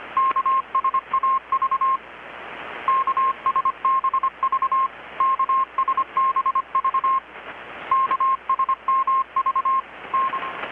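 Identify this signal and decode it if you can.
Morse code, heard as a keyed tone, about 25 wpm, 1060 Hz. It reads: KSAV KSBV KSBV KSMV N
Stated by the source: Morse from KPH's Night of Nights XX maritime radiotelegraphy event, received in Chiapas, Mexico about 3,800 km away on a Kenwood R-600 with a grounded 10 m long-wire antenna.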